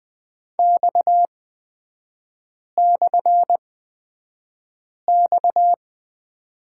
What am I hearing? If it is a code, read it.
Morse code, heard as a keyed tone, about 20 wpm, spaced as for 5 wpm, 690 Hz. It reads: X/X
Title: 229 Lesson 37e - Letter x - Three Characters 20wpm